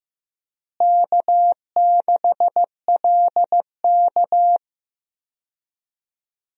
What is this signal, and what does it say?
Morse code, heard as a keyed tone, about 15 wpm, 700 Hz. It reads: K6LK
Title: USA Call Signs - 4 Characters - Repeated 3x - 15wpm